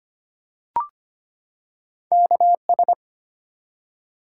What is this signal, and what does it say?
Morse code, heard as a keyed tone, about 26 wpm, 700 Hz. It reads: KS
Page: morse-code-ninja-practice-25wpm